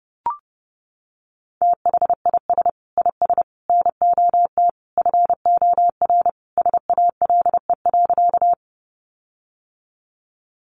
Morse code, heard as a keyed tone, about 30 wpm, 700 Hz. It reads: THIS IS NOT FOR SALE.
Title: Sentences from Top 1400 Words 30wpm